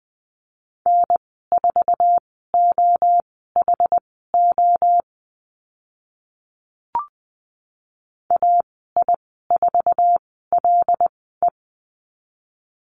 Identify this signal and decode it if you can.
Morse code, heard as a keyed tone, about 20 wpm, spaced as for 14 wpm, 700 Hz. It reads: N4OHO AI4LE
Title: 076 Lesson 13g - Letter L - Call Signs 20wpm